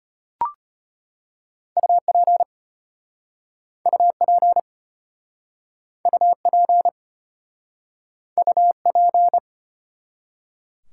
Morse code, heard as a keyed tone, about 33 wpm, 700 Hz. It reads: UP UP UP UP